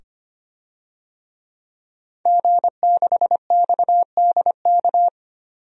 Morse code, heard as a keyed tone, about 25 wpm, 700 Hz. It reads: G6XDK
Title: DXCC-100 Call Signs - 5 Characters - 25wpm